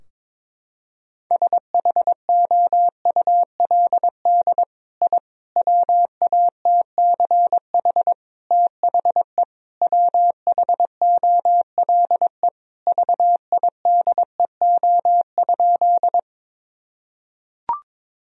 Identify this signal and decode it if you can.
Morse code, heard as a keyed tone, about 22 wpm, 700 Hz. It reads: SHOULD I WATCH THE WHOLE VIDEO?
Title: Sentences from Top 1000 Words - Review Part 4 - 22wpm